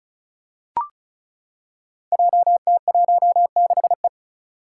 Morse code, heard as a keyed tone, about 36 wpm, 700 Hz. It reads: JT16E
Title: Five Challenging Letter-Number Combinations 35wpm